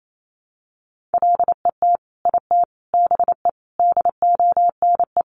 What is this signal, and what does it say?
Morse code, heard as a keyed tone, about 28 wpm, 700 Hz. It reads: LET IT BE DONE
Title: Sentences from Top 1100 Words 28wpm